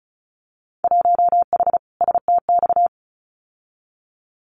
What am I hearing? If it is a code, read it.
Morse code, heard as a keyed tone, about 36 wpm, 690 Hz. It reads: 1H STX
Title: Winter Field Day Exchanges 35wpm